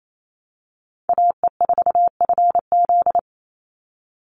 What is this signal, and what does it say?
Morse code, heard as a keyed tone, about 28 wpm, 700 Hz. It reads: AE4FZ
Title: NA SOTA Chaser Callsigns 28wpm